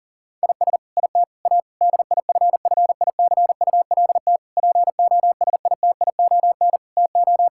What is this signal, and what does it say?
Morse code, heard as a keyed tone, about 40 wpm, 700 Hz. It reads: IS IT A DIFFICULT POSITION TO
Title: Sentences from Top 1000 Words - Review Part 5 - 40wpm